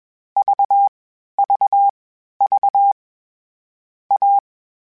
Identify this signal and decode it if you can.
Morse code, heard as a keyed tone, about 21 wpm, 800 Hz. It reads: V V V A
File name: Difficult Combos / Difficult Combos-mp3s / AWJ ShortRun@20